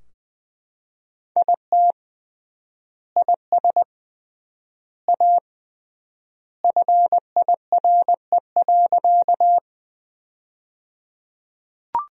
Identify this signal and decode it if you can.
Morse code, heard as a keyed tone, about 20 wpm, 700 Hz. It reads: IT IS A FIRE.
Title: 3x Word-Spacing - Sentences from Top 400 Words - 20wpm